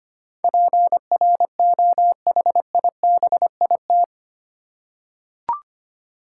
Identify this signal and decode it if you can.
Morse code, heard as a keyed tone, about 25 wpm, 700 Hz. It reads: PROHIBIT
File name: Eight Letter Words - 25wpm